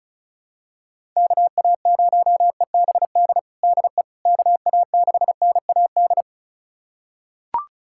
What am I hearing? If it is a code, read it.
Morse code, heard as a keyed tone, about 36 wpm, 700 Hz. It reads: KA0EBD DE KA6NAD
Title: USA Callsign DE Callsign - 6 Characters - 35wpm